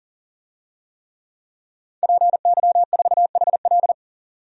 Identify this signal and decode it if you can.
Morse code, heard as a keyed tone, about 38 wpm, 700 Hz. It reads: PY4HL
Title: DXCC-100 Call Signs - 5 Characters - 40wpm